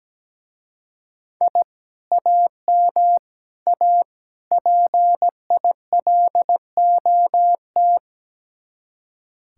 Morse code, heard as a keyed tone, about 17 wpm, 700 Hz. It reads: I AM A PILOT
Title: Speed-Racing Common QSO Phrases - 17wpm